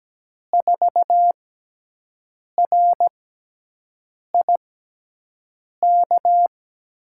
Morse code, heard as a keyed tone, about 17 wpm, spaced as for 6 wpm, 700 Hz. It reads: 4RIK
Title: USA Call Signs - 5 Characters - 6x Character Spacing - 17wpm